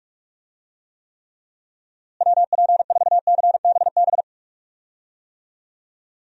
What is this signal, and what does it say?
Morse code, heard as a keyed tone, about 46 wpm, 700 Hz. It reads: WP4CBB
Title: USA Call Signs - 6 Characters - Repeated 3x - 45wpm